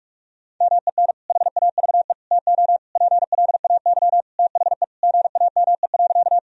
Morse code, heard as a keyed tone, about 41 wpm, 700 Hz. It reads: MEN HAVE TO PLAY THE GAME.
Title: Sentences from Top 1000 Words - Review Part 3 - 45wpm